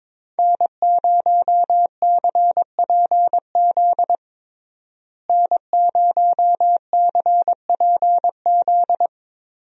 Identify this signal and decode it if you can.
Morse code, heard as a keyed tone, about 22 wpm, 700 Hz. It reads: N0CPZ N0CPZ
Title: USA Call Signs - 5 Characters - Repeated 3x - 22wpm